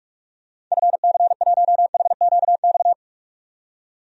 Morse code, heard as a keyed tone, about 46 wpm, 700 Hz. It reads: FC1HQX
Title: DXCC-100 Call Signs - 6 Characters - 45wpm